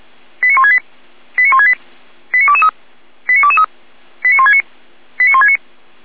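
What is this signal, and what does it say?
Five-tone selective call (ZVEI), heard as quick runs of five tones, tones of about 70 ms. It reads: 89168 89168 89223 89223 89169 89169
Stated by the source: ZVEI five-tone call 89073 received by SDR, channel 455.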